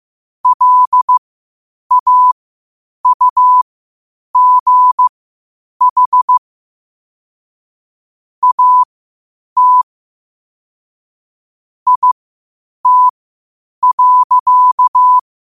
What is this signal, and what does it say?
Morse code, heard as a keyed tone, about 15 wpm, spaced as for 8 wpm, 1000 Hz. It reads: LAUGH AT IT.
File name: qotdfiles / 8wpm